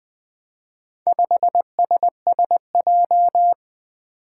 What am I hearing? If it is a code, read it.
Morse code, heard as a keyed tone, about 20 wpm, 700 Hz. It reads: 5SSJ
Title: Four Challenging Letter-Number Combinations 20wpm